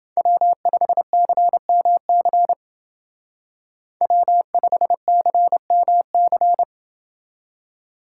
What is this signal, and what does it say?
Morse code, heard as a keyed tone, about 28 wpm, 700 Hz. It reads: W5CMC W5CMC